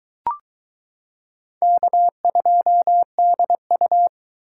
Morse code, heard as a keyed tone, about 22 wpm, 700 Hz. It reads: K2DU